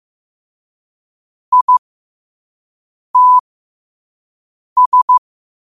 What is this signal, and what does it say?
Morse code, heard as a keyed tone, about 15 wpm, spaced as for 5 wpm, 1000 Hz. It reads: ITS
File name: qotdfiles / 5wpm